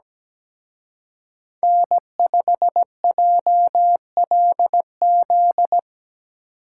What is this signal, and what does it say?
Morse code, heard as a keyed tone, about 19 wpm, 700 Hz. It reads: N5JLZ
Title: Speed Racing - USA 1x3 Call Signs 17wpm